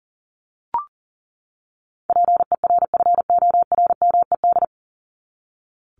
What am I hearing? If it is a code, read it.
Morse code, heard as a keyed tone, about 37 wpm, 700 Hz. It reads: PERFORMED
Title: ED Suffix - Top 100 Words with Base Words 40wpm